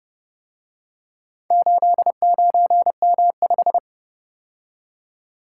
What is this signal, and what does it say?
Morse code, heard as a keyed tone, about 30 wpm, 700 Hz. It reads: 89M5